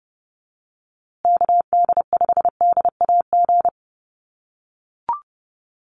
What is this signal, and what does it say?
Morse code, heard as a keyed tone, about 30 wpm, 700 Hz. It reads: KD5DAG
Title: USA Call Signs Repeated 2x 30wpm